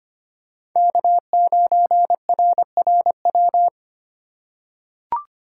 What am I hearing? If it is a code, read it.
Morse code, heard as a keyed tone, about 25 wpm, 700 Hz. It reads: K9RRW
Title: USA Call Signs - 5 Characters - 2x Character Spacing - 25wpm